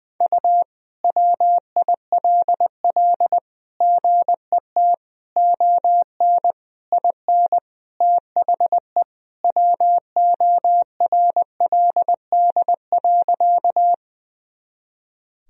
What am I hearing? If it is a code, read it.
Morse code, heard as a keyed tone, about 20 wpm, 700 Hz. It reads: U WILL GET ON IN THE WORLD.
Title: Sentences from Top 200 Words and Abbreviations 20wpm